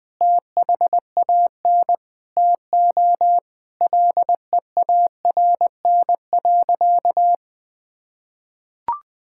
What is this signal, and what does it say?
Morse code, heard as a keyed tone, about 20 wpm, 700 Hz. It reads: THAN TO LEARN.
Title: Sentences with 7-Word Phrases - 20wpm